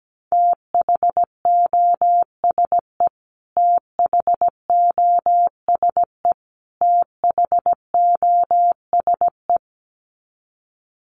Morse code, heard as a keyed tone, about 17 wpm, 700 Hz. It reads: THOSE THOSE THOSE
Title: Top 500 Words Repeated 17wpm